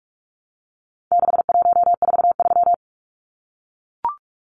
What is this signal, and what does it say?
Morse code, heard as a keyed tone, about 44 wpm, 700 Hz. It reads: 6143
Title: Numbers - 4-Digit 45wpm